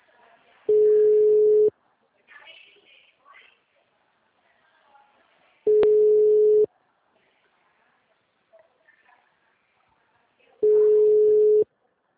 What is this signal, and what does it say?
Ringback tone (Europe, 425 Hz), with speech.
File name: out- -2019- - - .5208